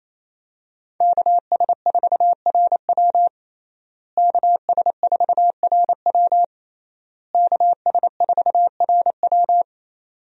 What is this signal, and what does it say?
Morse code, heard as a keyed tone, about 28 wpm, 700 Hz. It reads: KS4RW KS4RW KS4RW
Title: USA Call Signs - 5 Characters - Repeated 3x - 28wpm